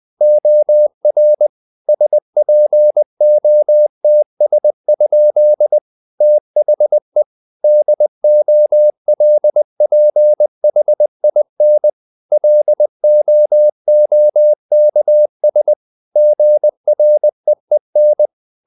Morse code, heard as a keyed tone, about 20 wpm, 600 Hz. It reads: OR SPOTS? THE DOLPHIN LOOKS GREEN
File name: the old man and the sea part 2 20wpm